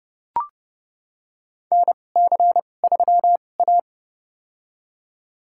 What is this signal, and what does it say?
Morse code, heard as a keyed tone, about 30 wpm, spaced as for 22 wpm, 700 Hz. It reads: NC3A
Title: USA Call Signs - 4 Characters - 2x Character Spacing - 30wpm